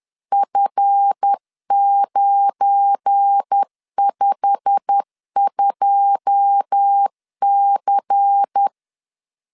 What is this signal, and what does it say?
Morse code, heard as a keyed tone, about 10 wpm, 800 Hz. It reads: F952C
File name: LEZIONI / 9 2 10wpm 2736